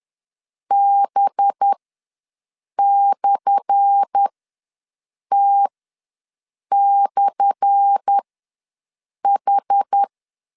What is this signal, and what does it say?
Morse code, heard as a keyed tone, about 10 wpm, spaced as for 6 wpm, 800 Hz. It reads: B/T/H
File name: LEZIONI / 10 SIGNS / 10A SLASH ws 10wpm 4121